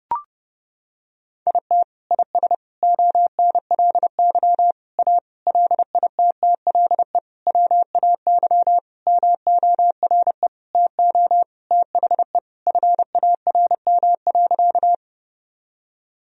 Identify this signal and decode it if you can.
Morse code, heard as a keyed tone, about 30 wpm, 700 Hz. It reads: IT IS ONLY A LITTLE WAY MORE TO THE FARM.
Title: Sentences from Top 1000 Words - Review Part 2 - 30wpm